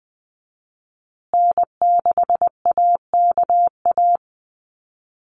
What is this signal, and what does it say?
Morse code, heard as a keyed tone, about 20 wpm, 700 Hz. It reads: N6AKA
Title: USA Call Signs Repeated 2x 20wpm